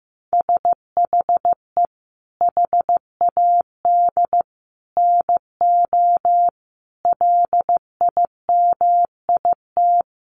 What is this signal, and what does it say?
Morse code, heard as a keyed tone, about 14 wpm, 700 Hz. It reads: SHE HAD NO LIMIT
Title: Sentences from Top 1500 Words 15wpm